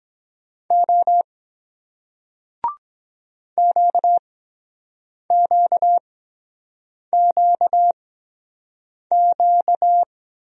Morse code, heard as a keyed tone, about 23 wpm, spaced as for 7 wpm, 700 Hz. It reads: O QQQQ